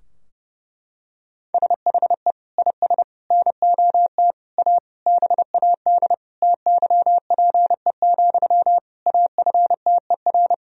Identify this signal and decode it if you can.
Morse code, heard as a keyed tone, about 32 wpm, 700 Hz. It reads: SHE IS NOT A BAD TYPE, AFTER
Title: Sentences from Top 1000 Words - Review Part 3 - 30wpm